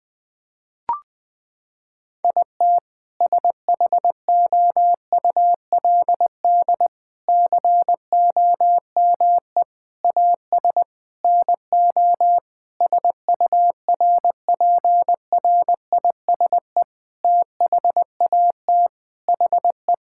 Morse code, heard as a keyed tone, about 19 wpm, 700 Hz. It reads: IT SHOULD COME AS NO SURPRISE THAT HE
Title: Sentences with 6-Word Phrases - 20wpm